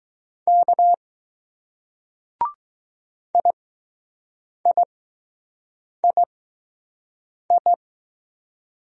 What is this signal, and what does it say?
Morse code, heard as a keyed tone, about 20 wpm, spaced as for 6 wpm, 700 Hz. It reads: K IIII